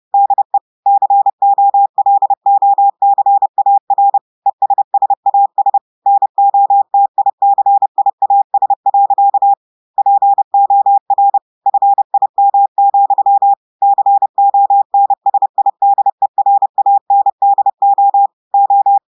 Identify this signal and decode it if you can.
Morse code, heard as a keyed tone, about 30 wpm, 800 Hz. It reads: DE COLOCAR ESSAS NOTICIAS. POR FIM, CONSIDERANDO O
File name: audios / portugais / portugais texte en clair 2 vitesse 30 mots minutes